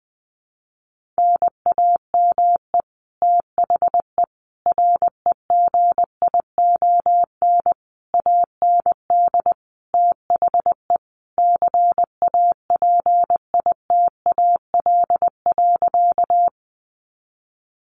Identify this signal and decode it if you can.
Morse code, heard as a keyed tone, about 20 wpm, 700 Hz. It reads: NAME THE REGION AND THE CAPITAL.
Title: Sentences from Top 1500 Words 20wpm